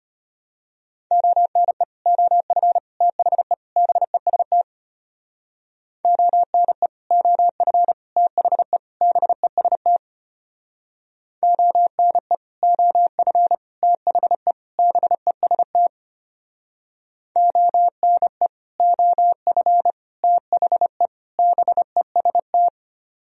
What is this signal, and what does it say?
Morse code, heard as a keyed tone, about 31 wpm, 700 Hz. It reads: ONE OF THE BEST ONE OF THE BEST ONE OF THE BEST ONE OF THE BEST